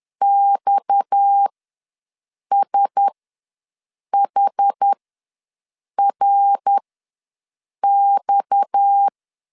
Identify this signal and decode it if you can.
Morse code, heard as a keyed tone, about 10 wpm, spaced as for 6 wpm, 800 Hz. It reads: XSHRX
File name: LEZIONI / 6 PKX0 / 6 PKX ws 10wpm 3706